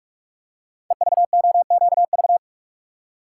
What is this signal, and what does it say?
Morse code, heard as a keyed tone, about 44 wpm, 690 Hz. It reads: EVOQV
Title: Code Groups - 5 Letters - 45wpm